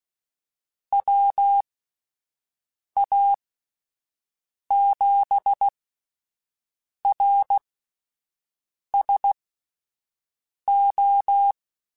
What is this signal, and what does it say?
Morse code, heard as a keyed tone, about 16 wpm, spaced as for 5 wpm, 800 Hz. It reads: WA7RSO
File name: Misc / Misc-mp3s / QST QST @10